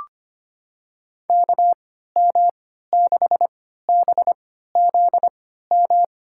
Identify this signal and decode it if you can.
Morse code, heard as a keyed tone, about 25 wpm, spaced as for 14 wpm, 700 Hz. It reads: KM6BZM
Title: USA Call Signs - 6 Characters - 3x Character Spacing - 25wpm